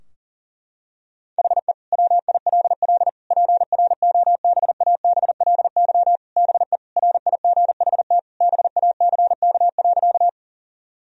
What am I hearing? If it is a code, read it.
Morse code, heard as a keyed tone, about 40 wpm, 700 Hz. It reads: HE WILL PROBABLY BE RIGHT BACK.
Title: Sentences from Top 1000 Words - Review Part 4 - 40wpm